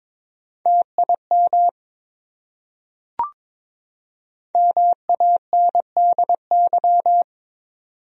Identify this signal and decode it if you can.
Morse code, heard as a keyed tone, about 21 wpm, 700 Hz. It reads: TIM MANDY
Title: USA Common Names - Max 5 Letters - 22wpm